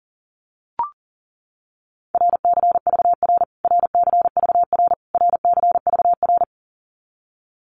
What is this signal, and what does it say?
Morse code, heard as a keyed tone, about 40 wpm, 700 Hz. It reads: RCVR RCVR RCVR
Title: Abbreviations Repeated 3x - 40wpm